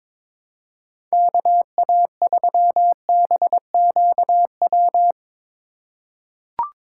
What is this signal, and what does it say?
Morse code, heard as a keyed tone, about 22 wpm, 700 Hz. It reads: KA3BQW
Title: USA Call Signs - 6 Characters - 4x Character Spacing - 22wpm